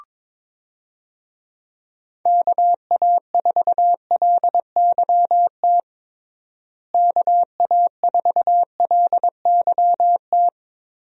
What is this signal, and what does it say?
Morse code, heard as a keyed tone, about 22 wpm, 700 Hz. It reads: KA4LYT KA4LYT
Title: USA Call Signs - 6 Characters - Repeated 3x - 22wpm